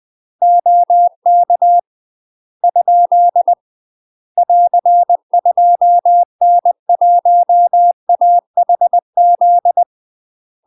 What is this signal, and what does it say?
Morse code, heard as a keyed tone, about 20 wpm, 700 Hz. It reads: OK ? <AR> 2N1AHZ
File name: UK MASTER QSOs TRAINING / 20WPM / UK MASTER QSOs 0040